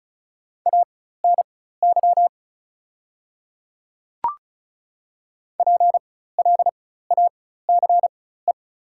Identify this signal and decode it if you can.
Morse code, heard as a keyed tone, about 36 wpm, spaced as for 16 wpm, 700 Hz. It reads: ANY PLACE
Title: 4x Character Spacing - Top 200 Words - 35wpm